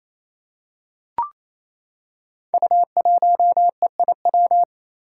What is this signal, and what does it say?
Morse code, heard as a keyed tone, about 29 wpm, 700 Hz. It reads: U1EIW
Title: Five Challenging Letter-Number Combinations 28wpm